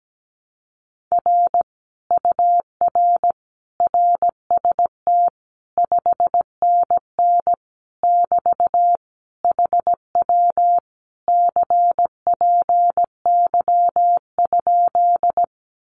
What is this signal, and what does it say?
Morse code, heard as a keyed tone, about 17 wpm, 700 Hz. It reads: R UR RST 5NN <BT> HW CPY?
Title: Common QSO Phrases Encore 17wpm